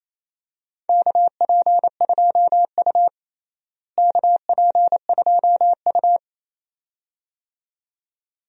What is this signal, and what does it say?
Morse code, heard as a keyed tone, about 28 wpm, 700 Hz. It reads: KP2U KP2U
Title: DXCC-100 Call Signs - 4 Characters - Repeated 2x - 28wpm